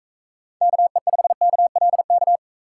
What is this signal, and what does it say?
Morse code, heard as a keyed tone, about 45 wpm, 700 Hz. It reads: KE5KLK